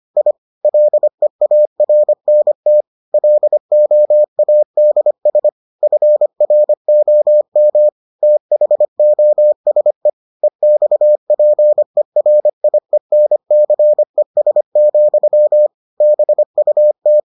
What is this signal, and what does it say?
Morse code, heard as a keyed tone, about 25 wpm, 600 Hz. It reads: I LEARNT LOADS FROM THOSE EXPERIENCES, BUT